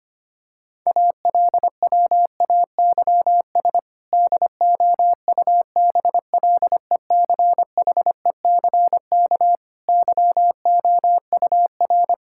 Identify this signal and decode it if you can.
Morse code, heard as a keyed tone, about 25 wpm, 700 Hz. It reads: ALWAYS DOUBLECHECK YOUR
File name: practice / 25wpm / Difficult Sentences with Hints - 25wpm